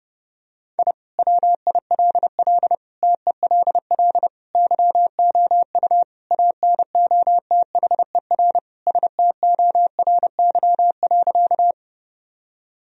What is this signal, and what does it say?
Morse code, heard as a keyed tone, about 30 wpm, 700 Hz. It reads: I WILL TELL YOU ANOTHER STORY.
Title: Sentences from Top 1000 Words - Review Part 1 - 30wpm